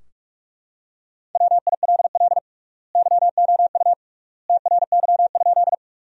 Morse code, heard as a keyed tone, about 41 wpm, 700 Hz. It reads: WILL YOU TRY?
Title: 3x Word-Spacing - Sentences from Top 200 Words - 45wpm